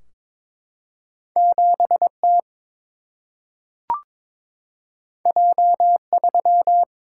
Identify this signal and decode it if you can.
Morse code, heard as a keyed tone, about 22 wpm, 700 Hz. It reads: 7 T J 3
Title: Single-Letter Single-Number 22wpm